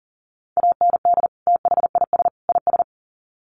Morse code, heard as a keyed tone, about 40 wpm, 700 Hz. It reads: AND THIS IS
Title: Sets of 3 Words 40wpm